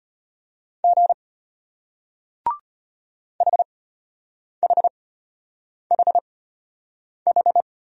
Morse code, heard as a keyed tone, about 33 wpm, spaced as for 8 wpm, 700 Hz. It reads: G HHHH